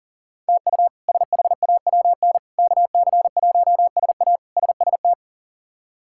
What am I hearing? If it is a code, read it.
Morse code, heard as a keyed tone, about 40 wpm, 700 Hz. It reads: TU SHAWN KC1SA SST
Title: K1USN SST QSO Practice - 40wpm